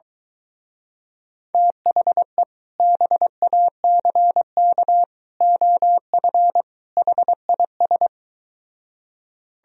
Morse code, heard as a keyed tone, about 23 wpm, 700 Hz. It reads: THE BACK OF HIS